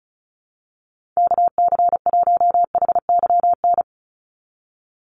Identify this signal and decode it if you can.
Morse code, heard as a keyed tone, about 34 wpm, 700 Hz. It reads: KC1HYN
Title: USA Call Signs 35wpm